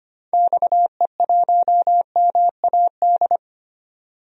Morse code, heard as a keyed tone, about 25 wpm, 700 Hz. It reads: XE1MAD
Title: DXCC-100 Call Signs - 6 Characters - 25wpm